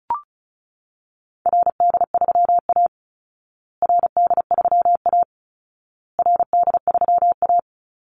Morse code, heard as a keyed tone, about 34 wpm, 700 Hz. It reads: RD3A RD3A RD3A
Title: Call Signs Repeated 2x - 35wpm